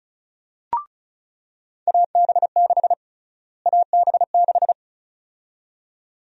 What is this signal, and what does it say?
Morse code, heard as a keyed tone, about 34 wpm, 700 Hz. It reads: AB6 AB6
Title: USA Call Sign Prefixes - Repeated 2x - 35wpm